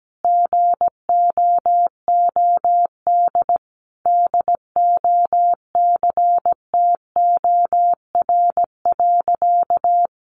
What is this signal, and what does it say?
Morse code, heard as a keyed tone, about 17 wpm, 700 Hz. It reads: GOOD DOCTOR.
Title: Sentences from Top 1500 Words 17wpm